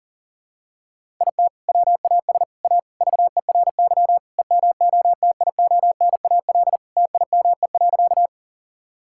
Morse code, heard as a keyed tone, about 37 wpm, 700 Hz. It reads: IT WAS A VERY EMOTIONAL TIME.
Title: Sentences from Top 1000 Words - Review Part 5 - 40wpm